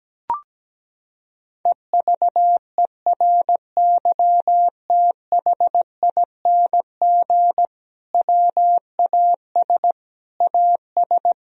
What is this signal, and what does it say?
Morse code, heard as a keyed tone, about 17 wpm, 700 Hz. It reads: EVERYTHING WAS AS
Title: Sentences from Top 1000 Words - Review Part 2 - 17wpm